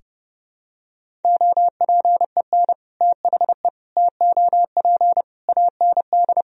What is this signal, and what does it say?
Morse code, heard as a keyed tone, about 30 wpm, 700 Hz. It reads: OPEN THE TOP AND
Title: Sentences from Top 1000 Words - Review Part 2 - 30wpm